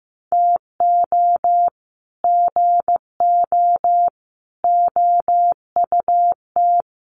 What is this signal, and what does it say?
Morse code, heard as a keyed tone, about 15 wpm, 700 Hz. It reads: TO GO OUT
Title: Sets of 5 Words Encore2 15wpm